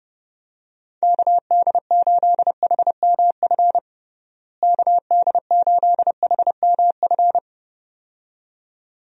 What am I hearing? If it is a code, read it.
Morse code, heard as a keyed tone, about 30 wpm, 700 Hz. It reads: KD8HMF KD8HMF
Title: USA Call Signs - 6 Characters - Repeated 2x - 30wpm